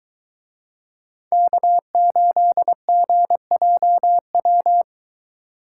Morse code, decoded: K8GJW

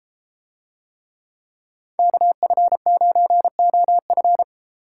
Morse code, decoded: KF9OF

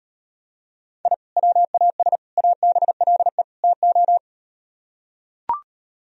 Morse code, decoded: I WAS ABLE TO

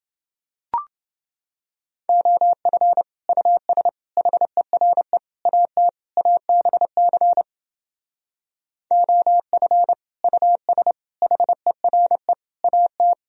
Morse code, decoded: OF US HERE AT ABC OF US HERE AT

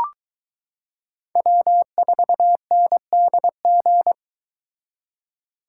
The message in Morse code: W4NDG

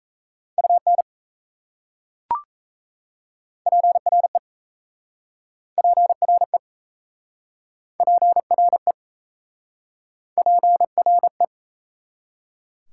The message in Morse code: UN PRE PRE PRE PRE

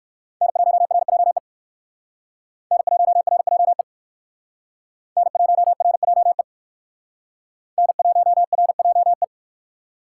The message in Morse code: N1RJE N1RJE N1RJE N1RJE

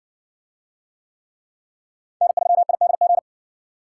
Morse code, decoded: N3IDG

